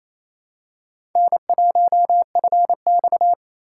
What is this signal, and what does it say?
Morse code, heard as a keyed tone, about 27 wpm, 700 Hz. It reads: N1FX
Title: USA Call Signs - 4 Characters - 4x Character Spacing - 28wpm